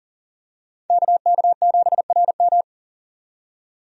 Morse code, heard as a keyed tone, about 40 wpm, 700 Hz. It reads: KK7RM
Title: USA Call Signs - 5 Characters - 40wpm